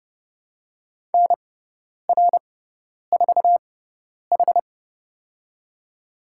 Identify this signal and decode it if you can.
Morse code, heard as a keyed tone, about 30 wpm, spaced as for 10 wpm, 700 Hz. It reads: NR4H
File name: Lesson 18g - Letter m - Call Signs 30wpm